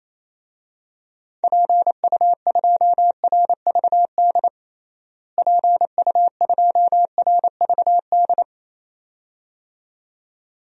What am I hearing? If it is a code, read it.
Morse code, heard as a keyed tone, about 27 wpm, 700 Hz. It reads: PU2RVD PU2RVD
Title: DXCC-100 Call Signs - 6 Characters - Repeated 2x - 28wpm